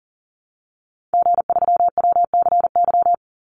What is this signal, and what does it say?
Morse code, heard as a keyed tone, about 40 wpm, 700 Hz. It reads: G3WCY
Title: Call Signs Repeated 3x - 40wpm